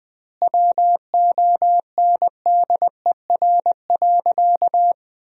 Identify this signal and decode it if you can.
Morse code, heard as a keyed tone, about 20 wpm, 700 Hz. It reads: WONDER.
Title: 2x Word-Spacing - Sentences from Top 500 Words - 20wpm